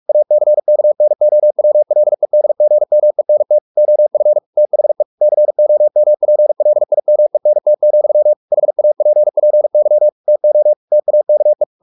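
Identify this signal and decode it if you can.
Morse code, heard as a keyed tone, about 44 wpm, 600 Hz. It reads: ACKNOWLEDGMENT OF THE COMPLIMENT, HAPPY TO TAKE
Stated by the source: Morse practice at 45 wpm, the novel Never by Ken Follett in English, from CW med Gnister.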